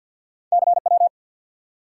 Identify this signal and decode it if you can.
Morse code, heard as a keyed tone, about 52 wpm, 700 Hz. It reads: CW
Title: Top 180 QSO Elements 50wpm